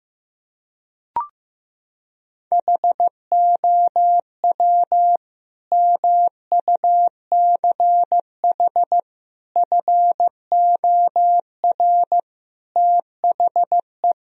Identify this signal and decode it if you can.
Morse code, heard as a keyed tone, about 15 wpm, 700 Hz. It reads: HOW MUCH FOR THE
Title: Sentences from Top 1000 Words - Review Part 4 - 15wpm